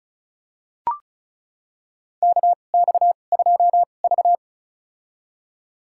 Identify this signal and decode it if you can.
Morse code, heard as a keyed tone, about 35 wpm, spaced as for 25 wpm, 700 Hz. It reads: KX2V
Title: USA Call Signs - 4 Characters - 2x Character Spacing - 35wpm